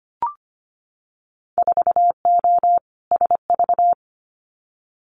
Morse code, heard as a keyed tone, about 25 wpm, 700 Hz. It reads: 4O SV
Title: Winter Field Day Exchanges 25wpm